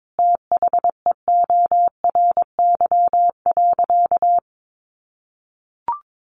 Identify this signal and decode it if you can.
Morse code, heard as a keyed tone, about 22 wpm, 700 Hz. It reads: THEORY.
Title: Sentences from Top 1400 Words 22wpm